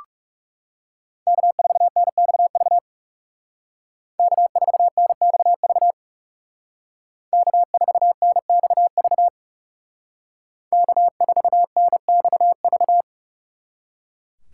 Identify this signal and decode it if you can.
Morse code, heard as a keyed tone, about 39 wpm, 700 Hz. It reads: K4NXV K4NXV K4NXV K4NXV